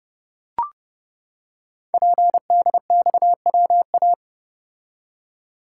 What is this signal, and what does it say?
Morse code, heard as a keyed tone, about 30 wpm, 700 Hz. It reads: PDXWA
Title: Code Groups - 5 Letter:Numbers - 30wpm